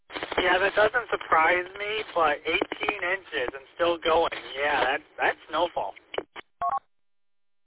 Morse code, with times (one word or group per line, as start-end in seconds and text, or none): none